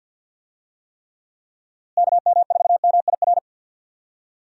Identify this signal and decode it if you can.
Morse code, heard as a keyed tone, about 50 wpm, 700 Hz. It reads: KM4MIR
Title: USA Call Signs - 6 Characters - Repeated 2x - 50wpm